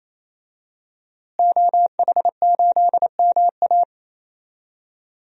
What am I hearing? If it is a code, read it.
Morse code, heard as a keyed tone, about 28 wpm, 700 Hz. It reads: OH8MA
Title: DXCC-100 Call Signs - 5 Characters - Repeated 3x - 28wpm